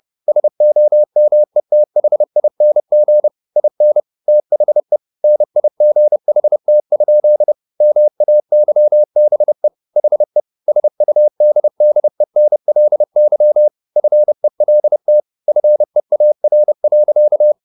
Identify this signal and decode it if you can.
Morse code, heard as a keyed tone, about 30 wpm, 600 Hz. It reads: SOMETHING IN THE NIGHT? MAYBE HE SUDDENLY FELT FEAR.